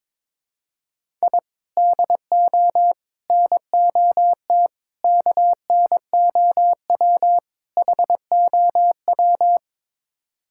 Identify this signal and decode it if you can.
Morse code, heard as a keyed tone, about 21 wpm, 700 Hz. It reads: I DO NOT KNOW HOW
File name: practice / 22wpm / Build A Phrase - 5-Word Phrases - 22wpm